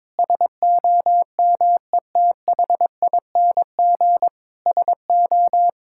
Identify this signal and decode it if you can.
Morse code, heard as a keyed tone, about 22 wpm, 700 Hz. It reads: SOMETHING SO
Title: Sentences from Top 1000 Words - Review Part 6 - 22wpm